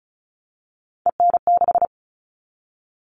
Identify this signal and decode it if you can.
Morse code, heard as a keyed tone, about 34 wpm, 700 Hz. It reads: EN6
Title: All Cut Numbers - 3-Digit 35wpm